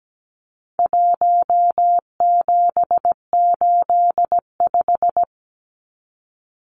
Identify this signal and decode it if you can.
Morse code, heard as a keyed tone, about 17 wpm, 700 Hz. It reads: 1785
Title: Numbers - 4-Digit 17wpm